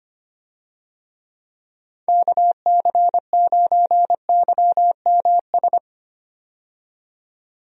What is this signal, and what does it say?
Morse code, heard as a keyed tone, about 25 wpm, 700 Hz. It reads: KC9YMS
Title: USA Call Signs - 6 Characters - 25wpm